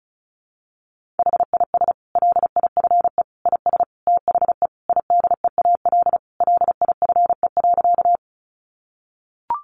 Morse code, heard as a keyed tone, about 34 wpm, 700 Hz. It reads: HIS LIFE IS THE IDEAL LIFE.
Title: Sentences from Top 1400 Words 35wpm